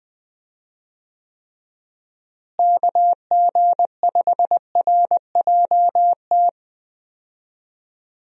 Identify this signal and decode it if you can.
Morse code, heard as a keyed tone, about 20 wpm, 700 Hz. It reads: KG5RJT
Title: USA Call Signs - 6 Characters - Repeated 2x - 20wpm